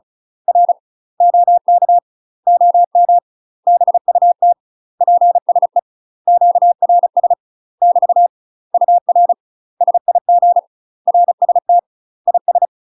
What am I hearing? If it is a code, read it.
Morse code, heard as a keyed tone, about 35 wpm, 700 Hz. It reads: R OK OM BUT PSE QRS <BT> UR SIG RST IS
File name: UK MASTER QSOs TRAINING / 35WPM / UK MASTER QSOs 0031